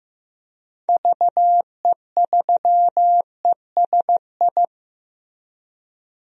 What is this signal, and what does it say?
Morse code, heard as a keyed tone, about 16 wpm, 700 Hz. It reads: VE3ESI
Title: DXCC-100 Call Signs - 6 Characters - Repeated 2x - 15wpm